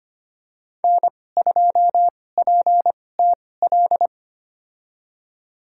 Morse code, heard as a keyed tone, about 25 wpm, spaced as for 18 wpm, 700 Hz. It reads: N2PTL
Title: USA Call Signs - 5 Characters - 2x Character Spacing - 25wpm